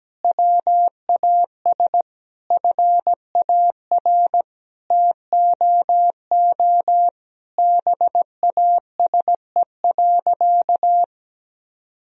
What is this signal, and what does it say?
Morse code, heard as a keyed tone, about 17 wpm, 700 Hz. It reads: WAS FAR TOO BASE.
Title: Sentences from Top 1000 Words - Review Part 1 - 17wpm